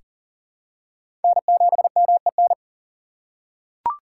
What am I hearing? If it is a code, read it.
Morse code, heard as a keyed tone, about 40 wpm, 700 Hz. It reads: N7MEN
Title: USA Call Signs - 5 Characters - 6x Character Spacing - 40wpm